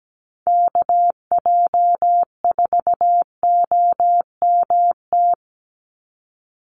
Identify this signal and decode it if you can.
Morse code, heard as a keyed tone, about 17 wpm, 700 Hz. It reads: KJ4OMT
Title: USA Call Signs 17wpm